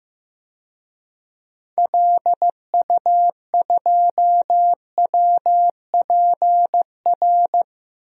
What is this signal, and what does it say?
Morse code, heard as a keyed tone, about 15 wpm, 700 Hz. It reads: LU2WPR
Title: DXCC-100 Call Signs - 6 Characters - Repeated 2x - 15wpm